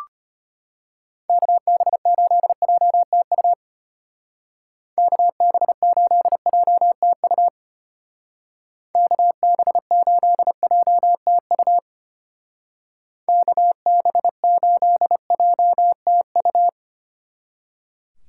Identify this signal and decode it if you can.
Morse code, heard as a keyed tone, about 30 wpm, 700 Hz. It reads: KB8JTU KB8JTU KB8JTU KB8JTU
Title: Speed Racing - USA 2x3 Call Signs 25wpm